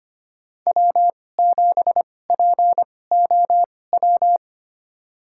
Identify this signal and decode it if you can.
Morse code, heard as a keyed tone, about 25 wpm, spaced as for 18 wpm, 700 Hz. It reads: W7POW
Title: USA Call Signs - 5 Characters - 2x Character Spacing - 25wpm